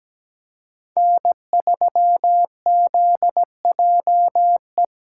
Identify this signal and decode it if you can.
Morse code, heard as a keyed tone, about 17 wpm, 700 Hz. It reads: N3ZJE
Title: USA Call Signs - 5 Characters - 4x Character Spacing - 17wpm